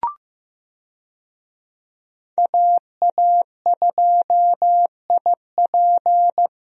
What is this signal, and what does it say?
Morse code, heard as a keyed tone, about 15 wpm, 700 Hz. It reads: AA2IP
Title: USA Call Signs - 5 Characters - 15wpm